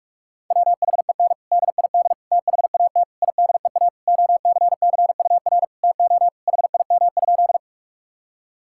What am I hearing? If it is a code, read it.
Morse code, heard as a keyed tone, about 46 wpm, 700 Hz. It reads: WHEN DID THAT IDEA OCCUR TO HIM?
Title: Sentences from Top 1000 Words - Review Part 6 - 45wpm